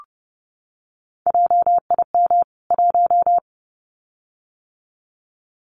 Morse code, heard as a keyed tone, about 30 wpm, 700 Hz. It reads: JIM 1
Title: CWT Exchange 30wpm